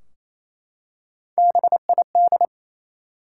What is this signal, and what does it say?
Morse code, heard as a keyed tone, about 27 wpm, 700 Hz. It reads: BID
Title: Three Letter Words - 28wpm